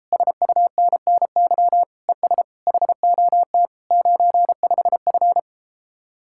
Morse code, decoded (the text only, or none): SUNNY ES HOT 95F